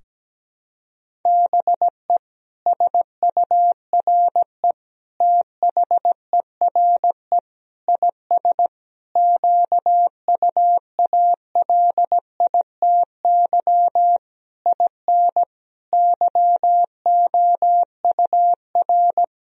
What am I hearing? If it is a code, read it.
Morse code, heard as a keyed tone, about 17 wpm, 700 Hz. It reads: BE SURE THERE IS QUALITY IN YOUR
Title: Sentences from Top 1000 Words - Review Part 4 - 17wpm